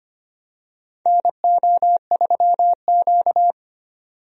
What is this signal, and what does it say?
Morse code, heard as a keyed tone, about 25 wpm, 700 Hz. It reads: NO3Q
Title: USA Call Signs - 4 Characters - 25wpm